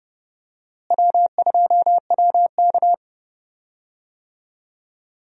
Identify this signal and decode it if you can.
Morse code, heard as a keyed tone, about 30 wpm, 700 Hz. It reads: W2WK
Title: USA Call Signs - 4 Characters - Repeated 3x - 30wpm